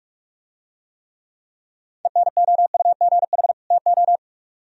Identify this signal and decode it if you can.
Morse code, heard as a keyed tone, about 49 wpm, 700 Hz. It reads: ENOUGH TO